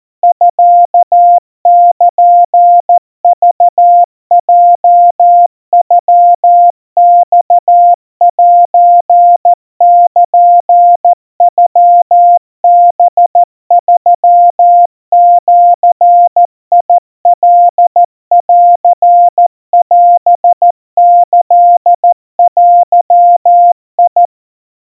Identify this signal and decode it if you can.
Morse code (Japanese Wabun code), heard as a keyed tone, about 14 wpm, 700 Hz. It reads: ミルクヲノマセルノハ3ジカンオキデ
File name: kawa / Morse sample / kamyu 50 cpm 700 Hz Crispy